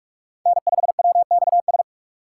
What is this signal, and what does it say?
Morse code, heard as a keyed tone, about 43 wpm, 700 Hz. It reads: N5WXS
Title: USA Call Signs - 3x Character Spacing - 45wpm